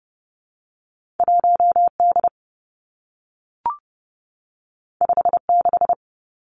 Morse code, heard as a keyed tone, about 30 wpm, 700 Hz. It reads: 1D 56
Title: Two Challenging Letter-Number Combinations 30wpm